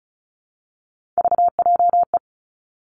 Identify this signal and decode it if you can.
Morse code, heard as a keyed tone, about 36 wpm, 700 Hz. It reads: VJE
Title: Three Challenging Letter-Number Combinations 35wpm